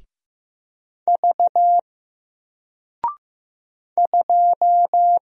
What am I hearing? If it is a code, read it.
Morse code, heard as a keyed tone, about 15 wpm, 700 Hz. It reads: V 2
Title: Single Letter-Number 15wpm